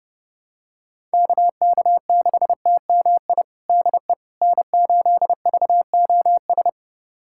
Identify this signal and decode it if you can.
Morse code, heard as a keyed tone, about 32 wpm, 700 Hz. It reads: KK6TMI DE N8VOS
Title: USA Callsign DE Callsign 30wpm